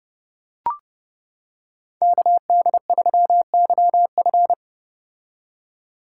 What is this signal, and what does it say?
Morse code, heard as a keyed tone, about 30 wpm, 700 Hz. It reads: KD3YF